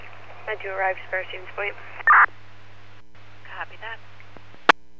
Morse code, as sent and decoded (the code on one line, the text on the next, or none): none